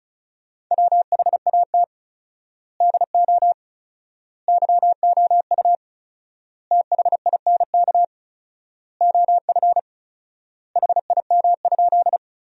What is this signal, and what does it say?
Morse code, heard as a keyed tone, about 36 wpm, 700 Hz. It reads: WHAT DO YOU THINK OF HIM?
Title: 4x Word-Spacing - Sentences from Top 200 Words - 35wpm